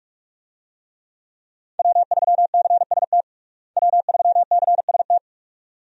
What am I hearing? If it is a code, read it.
Morse code, heard as a keyed tone, about 41 wpm, 700 Hz. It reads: W3CST W3CST
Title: USA Call Signs - 5 Characters - Repeated 2x - 45wpm